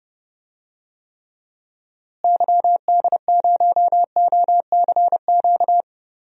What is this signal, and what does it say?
Morse code, heard as a keyed tone, about 30 wpm, 700 Hz. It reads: YD0OCQ
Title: DXCC-100 Call Signs - 6 Characters - Repeated 3x - 30wpm